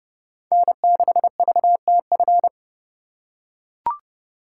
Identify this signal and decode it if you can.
Morse code, heard as a keyed tone, about 31 wpm, 700 Hz. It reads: N6VTF